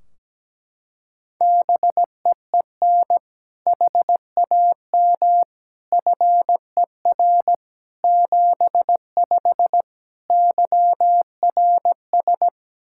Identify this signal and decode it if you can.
Morse code, heard as a keyed tone, about 17 wpm, 700 Hz. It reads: BEEN HAM FER 75 YRS
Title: Practice QSO - Exchange 3 - More Info - 17wpm